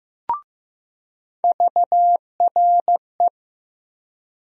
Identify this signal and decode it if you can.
Morse code, heard as a keyed tone, about 15 wpm, 700 Hz. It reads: VRE
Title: USA Call Sign Postfixes - 3 Letters 15wpm